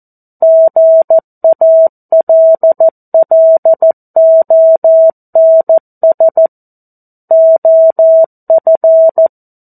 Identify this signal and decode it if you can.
Morse code, heard as a keyed tone, about 14 wpm, 650 Hz. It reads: GALLONS OF
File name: 02-Chapter-02 14 WPM